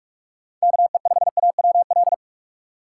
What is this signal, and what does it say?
Morse code, heard as a keyed tone, about 44 wpm, 700 Hz. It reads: KE5AWL
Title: USA Call Signs - 6 Characters - 6x Character Spacing - 45wpm